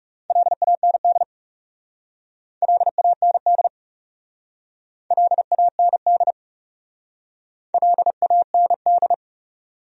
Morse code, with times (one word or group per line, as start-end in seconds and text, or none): 0.3-1.2 LAND
2.6-3.7 LAND
5.1-6.3 LAND
7.7-9.1 LAND